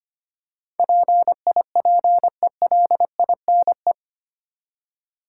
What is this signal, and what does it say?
Morse code, heard as a keyed tone, about 25 wpm, 700 Hz. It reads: PIPELINE
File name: Eight Letter Words - 25wpm